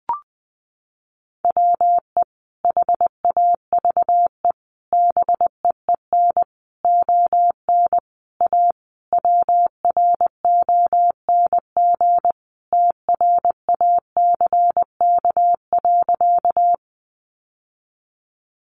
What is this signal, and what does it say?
Morse code, heard as a keyed tone, about 20 wpm, 700 Hz. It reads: WE HAVE BEEN ON A WRONG TRACK.
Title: Sentences from Top 1100 Words 20wpm